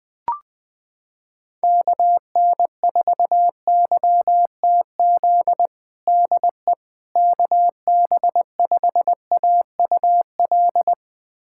Morse code, heard as a keyed tone, about 21 wpm, 700 Hz. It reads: KN4YTZ DE KB5AUL
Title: USA Callsign DE Callsign - 6 Characters - 20wpm